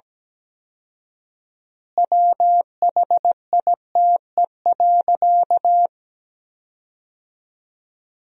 Morse code, heard as a keyed tone, about 17 wpm, 700 Hz. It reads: WHITE.
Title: 4x Word-Spacing - Sentences from Top 300 Words - 17wpm